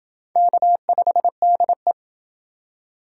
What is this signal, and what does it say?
Morse code, heard as a keyed tone, about 28 wpm, 700 Hz. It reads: K5DE